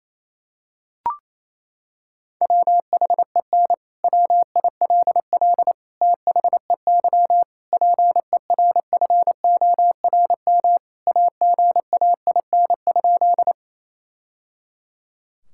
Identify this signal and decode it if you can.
Morse code, heard as a keyed tone, about 27 wpm, 700 Hz. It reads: WHEN WILL THEY PERFORM AGAIN?
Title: Sentences from Top 1000 Words - Review Part 5 - 28wpm